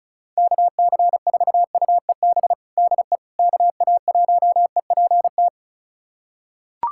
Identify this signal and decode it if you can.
Morse code, heard as a keyed tone, about 34 wpm, 700 Hz. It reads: KC4UEB DE KA1EPT
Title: USA Callsign DE Callsign - 6 Characters - 35wpm